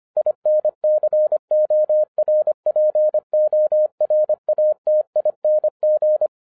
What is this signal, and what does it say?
Morse code, heard as a keyed tone, about 25 wpm, 610 Hz. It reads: INCORPORATING
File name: 2011-10-mp3 / n14-MORSE EXPRESS 2011 CHRISTMAS KEY 25wpm